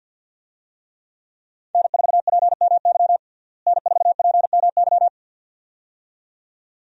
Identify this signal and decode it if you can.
Morse code, heard as a keyed tone, about 53 wpm, 700 Hz. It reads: N4PMY N4PMY
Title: USA Call Signs - 5 Characters - Repeated 2x - 50wpm